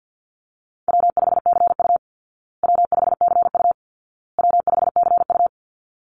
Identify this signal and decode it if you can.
Morse code, heard as a keyed tone, about 50 wpm, 700 Hz. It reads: W5CU W5CU W5CU
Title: Easy Call Signs Repeated 3x - 50wpm